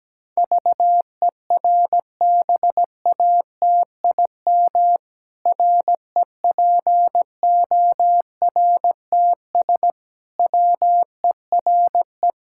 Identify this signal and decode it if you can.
Morse code, heard as a keyed tone, about 17 wpm, 700 Hz. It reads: VERBATIM REPORTS WERE
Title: Pangram Challenge v2 - 17wpm